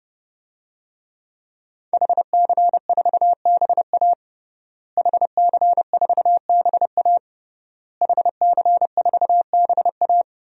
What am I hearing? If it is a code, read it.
Morse code, heard as a keyed tone, about 30 wpm, 700 Hz. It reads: HC4BA HC4BA HC4BA
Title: DXCC-100 Call Signs - 5 Characters - Repeated 3x - 30wpm